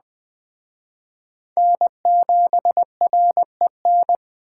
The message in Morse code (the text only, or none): N7REN